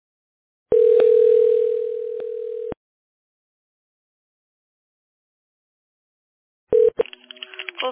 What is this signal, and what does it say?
Ringback tone (North America), 440 + 480 Hz.